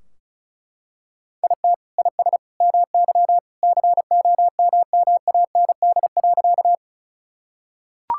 Morse code, heard as a keyed tone, about 36 wpm, 700 Hz. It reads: IT IS MY COMMAND.
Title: Sentences from Top 1600 Words 35wpm